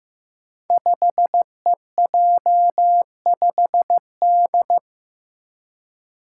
Morse code, heard as a keyed tone, about 15 wpm, 700 Hz. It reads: 5EJ5D
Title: Five Challenging Letter-Number Combinations 15wpm